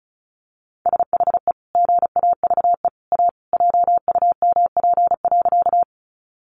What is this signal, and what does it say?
Morse code, heard as a keyed tone, about 34 wpm, 700 Hz. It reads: SHE GAVE A JUMP.
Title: Sentences from Top 1300 Words 35wpm